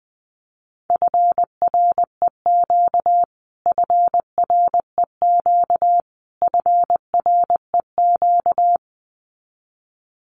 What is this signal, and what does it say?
Morse code, heard as a keyed tone, about 20 wpm, 700 Hz. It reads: FREQ FREQ FREQ
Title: Abbreviations Repeated 3x - 20wpm